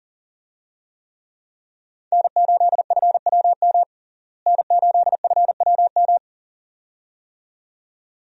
Morse code, heard as a keyed tone, about 40 wpm, 700 Hz. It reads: N8FWM N8FWM
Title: USA Call Signs - 5 Characters - Repeated 2x - 40wpm